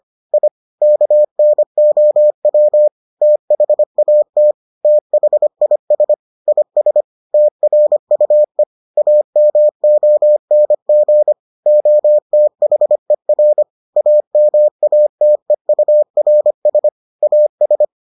Morse code, heard as a keyed tone, about 25 wpm, 600 Hz. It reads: I KNOW THAT THIS IS TRUE AMONG OTHER AMATEURS AS